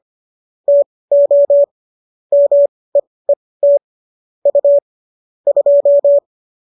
Morse code, heard as a keyed tone, about 25 wpm, spaced as for 18 wpm, 570 Hz. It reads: TO MEET U 2